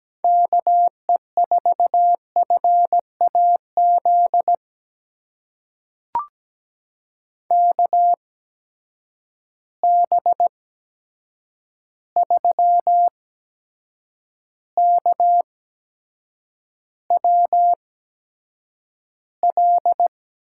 Morse code, heard as a keyed tone, about 17 wpm, spaced as for 7 wpm, 700 Hz. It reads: KE4FAZ KB3KWL
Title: USA Call Signs - 6 Characters - 8x Character Spacing - 17wpm